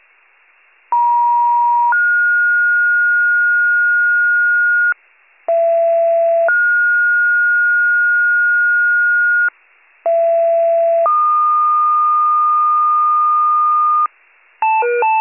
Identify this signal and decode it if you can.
Two-tone page, heard as a one-second tone, then a three-second tone, about 950 then 1430 Hz, about 670 then 1430 Hz, about 670 then 1220 Hz, with noise. It is heard from 0.9 to 14.1 s.